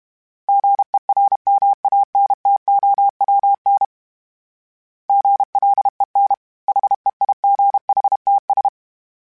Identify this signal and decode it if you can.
Morse code, heard as a keyed tone, about 32 wpm, 800 Hz. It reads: GERMANTOWN GLEN HEIGHTS
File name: fromSMs / FromSMs mp3s / Cities@30